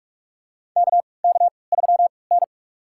Morse code, heard as a keyed tone, about 42 wpm, spaced as for 25 wpm, 700 Hz. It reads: KK3N
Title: USA Call Signs - 4 Characters - 3x Character Spacing - 45wpm